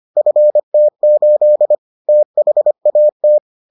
Morse code, heard as a keyed tone, about 25 wpm, 600 Hz. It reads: FT8 THAT